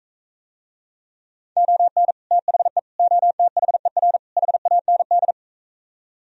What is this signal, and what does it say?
Morse code, heard as a keyed tone, about 45 wpm, 700 Hz. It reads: ON THE OTHER HAND